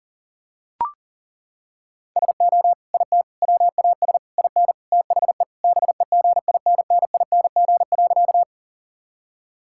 Morse code, heard as a keyed tone, about 40 wpm, 700 Hz. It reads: SO IT WAS IN THE BEGINNING.
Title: Sentences from Top 1000 Words - Review Part 5 - 40wpm